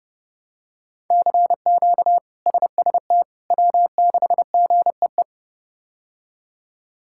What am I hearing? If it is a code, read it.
Morse code, heard as a keyed tone, about 30 wpm, 700 Hz. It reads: CQ SST W6GEE
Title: K1USN SST QSO Practice - 30wpm